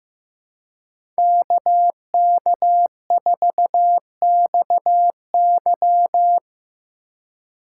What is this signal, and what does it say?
Morse code, heard as a keyed tone, about 15 wpm, 700 Hz. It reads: KK4XY
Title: USA Call Signs - 5 Characters - 4x Character Spacing - 15wpm